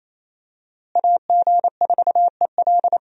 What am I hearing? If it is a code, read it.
Morse code, heard as a keyed tone, about 28 wpm, 700 Hz. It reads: AG4EL
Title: USA Call Signs - 5 Characters - Repeated 3x - 28wpm